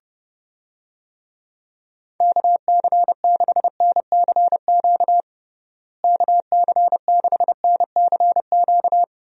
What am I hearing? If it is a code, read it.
Morse code, heard as a keyed tone, about 32 wpm, 700 Hz. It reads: KC6NCQ KC6NCQ
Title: USA Call Signs - 6 Characters - Repeated 2x - 30wpm